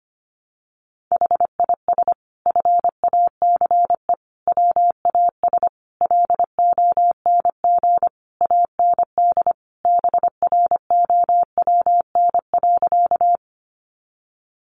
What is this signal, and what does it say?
Morse code, heard as a keyed tone, about 25 wpm, 700 Hz. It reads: HIS FACE WAS LONG AND BROWN.
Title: Sentences from Top 1200 Words 25wpm